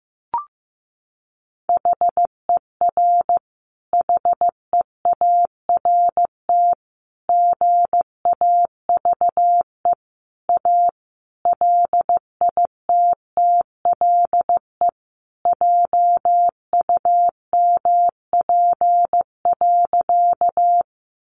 Morse code, heard as a keyed tone, about 15 wpm, 700 Hz. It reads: HER HEART GAVE A LITTLE JUMP.
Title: Sentences from Top 1300 Words 15wpm